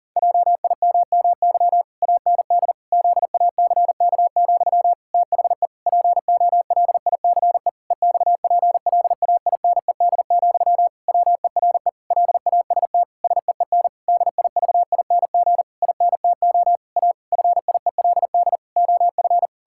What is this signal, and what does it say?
Morse code, heard as a keyed tone, about 37 wpm, 700 Hz. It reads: JIMMY AND ZACK, THE POLICE EXPLAINED, WERE LAST SEEN DIVING INTO A FIELD OF
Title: Pangram Challenge v2 - 40wpm